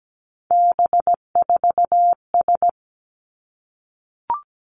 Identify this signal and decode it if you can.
Morse code, heard as a keyed tone, about 17 wpm, 700 Hz. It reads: B4S
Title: Three Challenging Letter-Number Combinations 17wpm